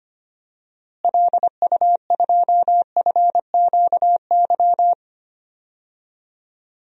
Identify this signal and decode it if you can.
Morse code, heard as a keyed tone, about 25 wpm, 700 Hz. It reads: LU2FQY
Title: DXCC-100 Call Signs - 6 Characters - Repeated 2x - 25wpm